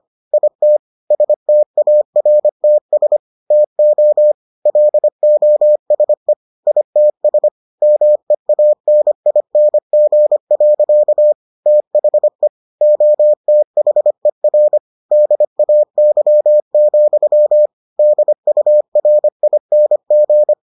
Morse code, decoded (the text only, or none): IT STARTS TO LOSE ITS MEANING. THE OTHER DAY, DURING